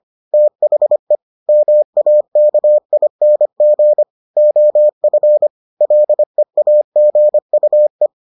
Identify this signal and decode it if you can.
Morse code, heard as a keyed tone, about 25 wpm, 600 Hz. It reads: THE MAKING OF LEAGUE